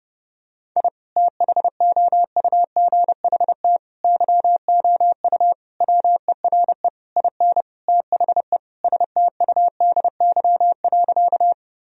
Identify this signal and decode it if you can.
Morse code, heard as a keyed tone, about 29 wpm, 700 Hz. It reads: I THOUGHT YOU WERE IN THE STUDY.
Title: Sentences from Top 1000 Words - Review Part 1 - 30wpm